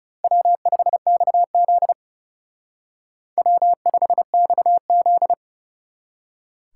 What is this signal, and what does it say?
Morse code, heard as a keyed tone, about 32 wpm, 700 Hz. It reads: W5XZ W5XZ
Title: Speed Racing - USA 1x2 Call Signs 30wpm